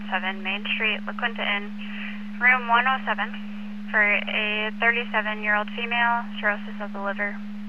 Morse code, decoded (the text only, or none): none